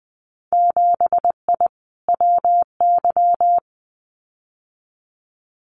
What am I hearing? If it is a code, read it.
Morse code, heard as a keyed tone, about 20 wpm, 700 Hz. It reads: 7I WY
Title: Winter Field Day Exchanges Encore 20wpm